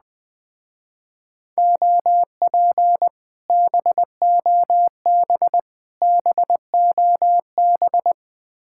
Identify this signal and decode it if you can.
Morse code, heard as a keyed tone, about 20 wpm, 700 Hz. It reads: OP BOB BOB
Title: Speed-Racing Common QSO Phrases Encore - 17wpm